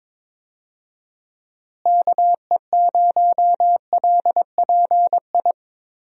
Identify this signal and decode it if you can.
Morse code, heard as a keyed tone, about 22 wpm, 700 Hz. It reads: KE0LPI
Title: USA Call Signs - 6 Characters - Repeated 3x - 22wpm